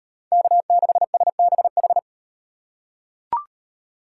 Morse code, heard as a keyed tone, about 38 wpm, 700 Hz. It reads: K6SBH